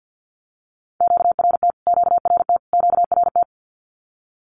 Morse code, heard as a keyed tone, about 50 wpm, 700 Hz. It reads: QRT QRT QRT
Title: Q-Codes Repeated 3x 50wpm